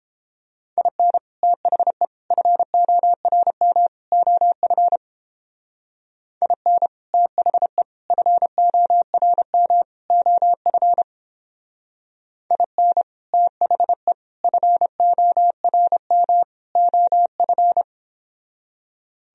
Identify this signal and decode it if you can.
Morse code, heard as a keyed tone, about 31 wpm, 700 Hz. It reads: IN THE FORM OF IN THE FORM OF IN THE FORM OF